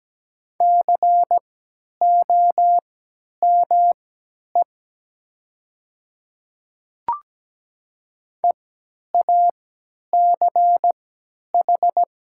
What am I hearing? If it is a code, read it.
Morse code, heard as a keyed tone, about 17 wpm, spaced as for 10 wpm, 690 Hz. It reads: COME EACH
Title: 3x Character Spacing - Top 200 Words - 17wpm